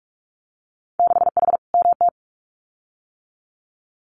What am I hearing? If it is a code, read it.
Morse code, heard as a keyed tone, about 44 wpm, 690 Hz. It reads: 6H MT
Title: Winter Field Day Exchanges Encore 45wpm